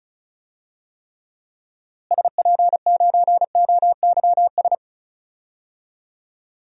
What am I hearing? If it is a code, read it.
Morse code, heard as a keyed tone, about 34 wpm, 700 Hz. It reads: SP9OYS
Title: DXCC-100 Call Signs - 6 Characters - Repeated 2x - 35wpm